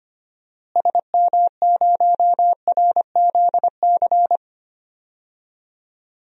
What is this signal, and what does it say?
Morse code, heard as a keyed tone, about 25 wpm, 700 Hz. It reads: SM0RZC
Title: DXCC-100 Call Signs - 6 Characters - Repeated 2x - 25wpm